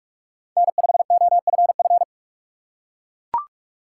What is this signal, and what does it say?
Morse code, heard as a keyed tone, about 42 wpm, 700 Hz. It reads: N5OFF
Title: USA Call Signs - 5 Characters - 6x Character Spacing - 45wpm